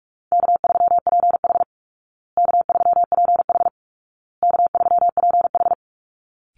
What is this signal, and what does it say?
Morse code, heard as a keyed tone, about 44 wpm, 700 Hz. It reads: K3PH K3PH K3PH
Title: Call Signs Repeated 2x - 45wpm